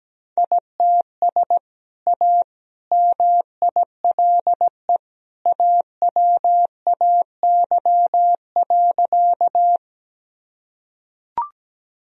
Morse code, heard as a keyed tone, about 17 wpm, 700 Hz. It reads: ITS A MILE AWAY.
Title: Sentences from Top 1000 Words - Review Part 1 - 17wpm